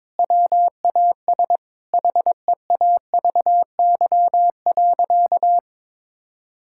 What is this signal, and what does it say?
Morse code, heard as a keyed tone, about 22 wpm, 700 Hz. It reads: WAS HEAVY.
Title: Sentences from Top 1600 Words 22wpm